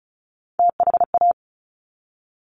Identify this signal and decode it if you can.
Morse code, heard as a keyed tone, about 36 wpm, 700 Hz. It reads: THA
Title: Top Three Letter Combinations in Words 35wpm